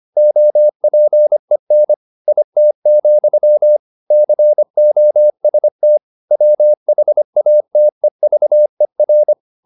Morse code, heard as a keyed tone, about 25 wpm, 600 Hz. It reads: OPEN IT, COST WHATEVER
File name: the old man and the sea part 1 25wpm